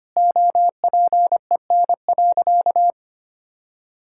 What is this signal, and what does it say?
Morse code, heard as a keyed tone, about 25 wpm, 700 Hz. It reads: OPEN.
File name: practice / 25wpm / All-Age Appropriate Jokes - 25wpm